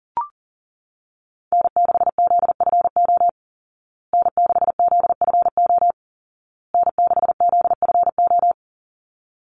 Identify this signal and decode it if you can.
Morse code, heard as a keyed tone, about 37 wpm, 700 Hz. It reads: N6ZFO N6ZFO N6ZFO
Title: Call Signs Repeated 2x - 40wpm